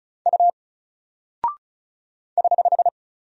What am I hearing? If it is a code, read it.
Morse code, heard as a keyed tone, about 36 wpm, 700 Hz. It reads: U <HH>
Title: Single Character (Letter:Number:Punctuation:Prosign) Alt 35wpm